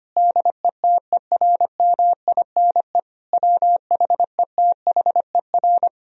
Morse code, heard as a keyed tone, about 25 wpm, 700 Hz. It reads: DETERMINE WHETHER
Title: Sentences from Top 1000 Words - Review Part 4 - 25wpm